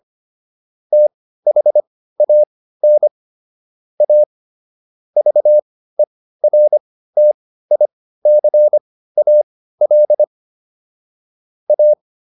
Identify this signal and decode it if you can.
Morse code, heard as a keyed tone, about 25 wpm, spaced as for 15 wpm, 600 Hz. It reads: THAN A VERTICAL A